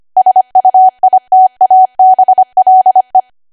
Morse code, heard as a keyed tone, about 25 wpm, 750 Hz. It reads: SUITABLE
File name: cwa / W1AW-Level3 / 25WPM